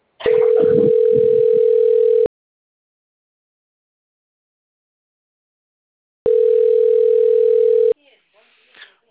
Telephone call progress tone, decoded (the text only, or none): ringback tone